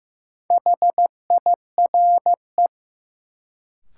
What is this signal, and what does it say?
Morse code, heard as a keyed tone, about 15 wpm, 700 Hz. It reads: HIRE
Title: Sentences from Top 1400 Words - New Word Drill - 15wpm